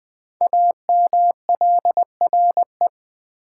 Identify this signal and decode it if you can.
Morse code, heard as a keyed tone, about 20 wpm, 700 Hz. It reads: AMLRE
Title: Code Groups - 5 Letters - 20wpm